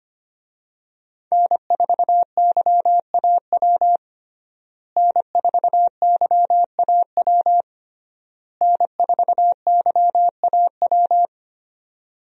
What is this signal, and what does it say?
Morse code, heard as a keyed tone, about 25 wpm, 700 Hz. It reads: N4YAW N4YAW N4YAW
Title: USA Call Signs - 5 Characters - Repeated 3x - 25wpm